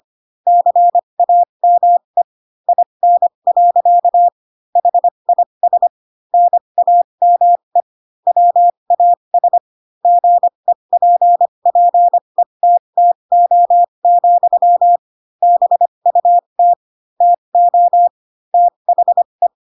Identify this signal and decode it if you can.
Morse code, heard as a keyed tone, about 25 wpm, spaced as for 22 wpm, 700 Hz. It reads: CAME IN. HIS NAME WAS GEPPETTO, BUT TO THE
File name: Pinocchio / Pinocchio ch2 18 25 ik1hge